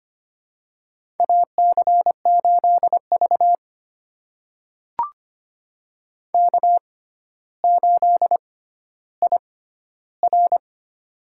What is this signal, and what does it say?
Morse code, heard as a keyed tone, about 25 wpm, spaced as for 13 wpm, 700 Hz. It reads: AC8V K8IR
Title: USA Call Signs - 4 Characters - 6x Character Spacing - 25wpm